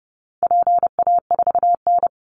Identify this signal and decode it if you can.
Morse code, heard as a keyed tone, about 30 wpm, 700 Hz. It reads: PA4N
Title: Call Signs Repeated 2x - 30wpm